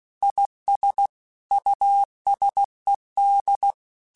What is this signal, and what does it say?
Morse code, heard as a keyed tone, about 16 wpm, 800 Hz. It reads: IS USED
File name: Technical Subjects / Technical Subjects-mp3s / Reactance @15